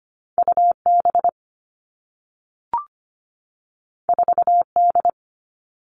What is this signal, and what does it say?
Morse code, heard as a keyed tone, about 25 wpm, 700 Hz. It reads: UB 4D
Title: All Cut Numbers - 2-Digit 25wpm